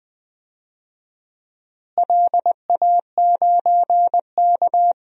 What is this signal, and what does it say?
Morse code, heard as a keyed tone, about 21 wpm, 700 Hz. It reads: LA9K
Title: DXCC-100 Call Signs - 4 Characters - Repeated 2x - 20wpm